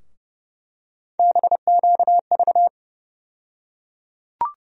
Morse code, heard as a keyed tone, about 29 wpm, 700 Hz. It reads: BQV
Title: USA Call Sign Postfixes - 3 Letters 30wpm